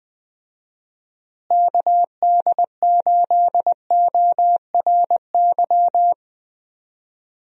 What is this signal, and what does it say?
Morse code, heard as a keyed tone, about 20 wpm, 700 Hz. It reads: KD8ORY
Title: USA Call Signs - 6 Characters - Repeated 2x - 20wpm